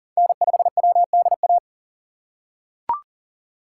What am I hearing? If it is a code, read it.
Morse code, heard as a keyed tone, about 40 wpm, 700 Hz. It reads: N5WDA
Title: USA Call Signs - 5 Characters - 6x Character Spacing - 40wpm